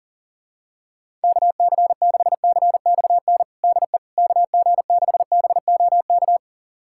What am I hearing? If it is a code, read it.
Morse code, heard as a keyed tone, about 40 wpm, 700 Hz. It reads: KC6CXN DE KG6BOK
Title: USA Callsign DE Callsign - 6 Characters - 40wpm